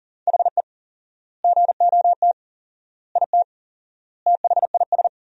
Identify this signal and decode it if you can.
Morse code, heard as a keyed tone, about 37 wpm, 700 Hz. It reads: HE GOT IT THIS
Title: 4x Word-Spacing - Sentences from Top 400 Words - 40wpm